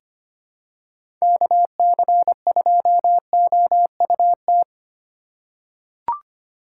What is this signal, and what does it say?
Morse code, heard as a keyed tone, about 25 wpm, 700 Hz. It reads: KC2OUT
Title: USA Call Signs - 8x Character Spacing - 25wpm